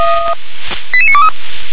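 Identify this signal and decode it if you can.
Five-tone selective call (ZVEI), heard as quick runs of five tones, tones of about 70 ms. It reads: 88923
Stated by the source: ZVEI five-tone call 88988 received by SDR.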